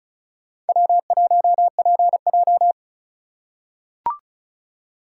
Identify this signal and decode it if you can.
Morse code, heard as a keyed tone, about 36 wpm, 700 Hz. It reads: W1PJ
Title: USA Call Signs - 4 Characters - 2x Character Spacing - 35wpm